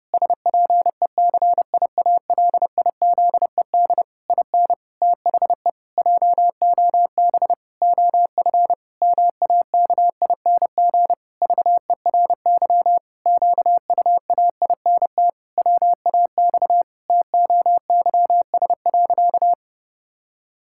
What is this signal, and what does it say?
Morse code, heard as a keyed tone, about 32 wpm, 700 Hz. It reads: SPECIALIZED IN THE JOB OF MAKING VERY QUAINT WAX TOYS.
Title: Pangram Challenge v2 - 30wpm